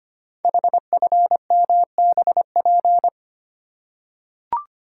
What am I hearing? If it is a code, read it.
Morse code, heard as a keyed tone, about 25 wpm, 700 Hz. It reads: HFMBP
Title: Code Groups - 5 Letter:Numbers - 25wpm